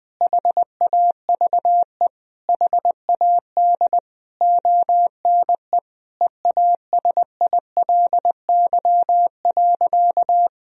Morse code, heard as a keyed tone, about 21 wpm, 700 Hz. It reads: HAVE HAD ONE EASILY.
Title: Sentences from Top 1000 Words - Review Part 3 - 20wpm